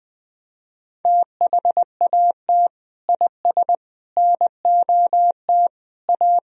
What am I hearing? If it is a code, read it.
Morse code, heard as a keyed tone, about 20 wpm, 700 Hz. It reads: THAT IS NOT A
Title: Sentences from Top 1000 Words - Review Part 4 - 20wpm